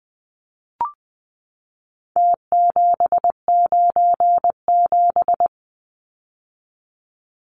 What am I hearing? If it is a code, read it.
Morse code, heard as a keyed tone, about 20 wpm, 700 Hz. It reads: T797
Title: Numbers - Contest Style - 4-Digit 20wpm